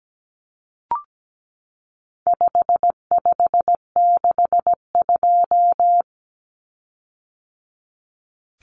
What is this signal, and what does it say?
Morse code, heard as a keyed tone, about 17 wpm, 700 Hz. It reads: 5562
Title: Common Cut Numbers - 4-Digit 17wpm